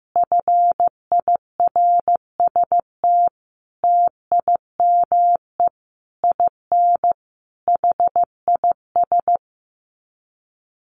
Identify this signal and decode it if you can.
Morse code, heard as a keyed tone, about 15 wpm, 700 Hz. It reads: FIRST TIME IN HIS
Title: Sets of 5 Words Encore2 15wpm